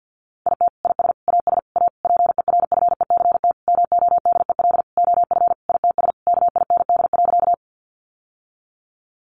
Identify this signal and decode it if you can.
Morse code, heard as a keyed tone, about 50 wpm, 700 Hz. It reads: IT IS AS A PERFECT MODEL OF ITS KIND.
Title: Sentences from Top 1000 Words 50wpm